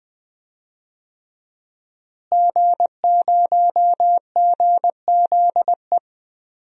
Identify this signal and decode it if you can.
Morse code, heard as a keyed tone, about 20 wpm, 700 Hz. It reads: G0GZE
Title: DXCC-100 Call Signs - 5 Characters - Repeated 3x - 20wpm